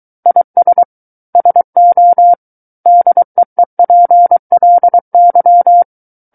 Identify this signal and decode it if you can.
Morse code, heard as a keyed tone, about 23 wpm, 700 Hz. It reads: IS SO DEEPLY
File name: Book One / 14-Chapter 14 - 23 WPM